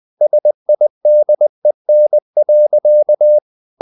Morse code, heard as a keyed tone, about 20 wpm, 600 Hz. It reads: SIDEN.
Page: Ferie 20 wpm